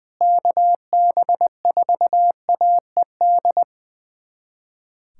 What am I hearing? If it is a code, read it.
Morse code, heard as a keyed tone, about 21 wpm, 700 Hz. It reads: KB4AED
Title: USA Callsign DE Callsign - 6 Characters - 20wpm